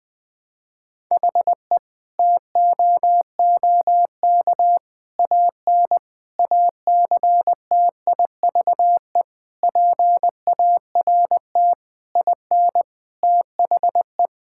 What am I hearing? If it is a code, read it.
Morse code, heard as a keyed tone, about 21 wpm, 700 Hz. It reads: HE TOOK AN ACTIVE PART IN THE
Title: Sentences from Top 1000 Words - Review Part 6 - 20wpm